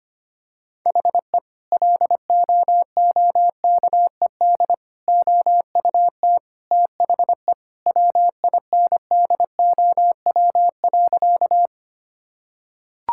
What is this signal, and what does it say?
Morse code, heard as a keyed tone, about 25 wpm, 700 Hz. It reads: HE LOOKED OUT THE WINDOW.
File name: practice / 25wpm / Sentences with 2-Word Phrases - 25wpm